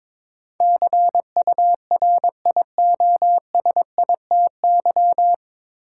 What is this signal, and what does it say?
Morse code, heard as a keyed tone, about 22 wpm, 700 Hz. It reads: CURIOSITY
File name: 9 Letter Words - First 3 Letters Given - 22wpm